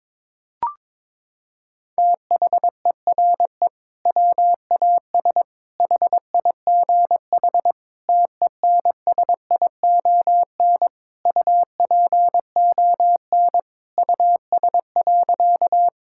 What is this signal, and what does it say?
Morse code, heard as a keyed tone, about 22 wpm, 700 Hz. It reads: THERE WAS HIGH TENSION UPON US.
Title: Sentences from Top 1600 Words 22wpm